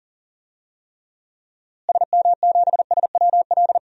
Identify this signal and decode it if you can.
Morse code, heard as a keyed tone, about 40 wpm, 700 Hz. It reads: SM7SWL
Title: DXCC-100 Call Signs - 6 Characters - Repeated 2x - 40wpm